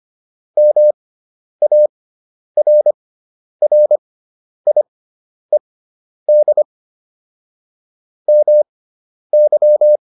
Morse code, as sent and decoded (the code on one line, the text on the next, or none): -- .- .-. .-. .. -.. / -- -.--
MARRIED MY